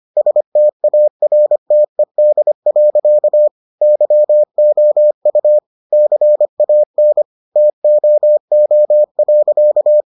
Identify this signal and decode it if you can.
Morse code, heard as a keyed tone, about 25 wpm, 600 Hz. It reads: STARTED. YOU CAN TOO.